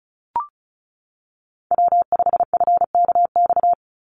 Morse code, heard as a keyed tone, about 34 wpm, 700 Hz. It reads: W5FKX
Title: Call Signs - 35wpm